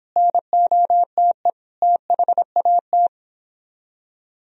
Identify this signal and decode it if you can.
Morse code, heard as a keyed tone, about 26 wpm, 700 Hz. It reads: NOTE THAT